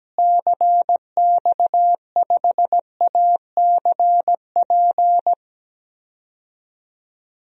Morse code, heard as a keyed tone, about 17 wpm, 700 Hz. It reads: CX5ACP
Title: DXCC-100 Call Signs - 6 Characters - Repeated 2x - 17wpm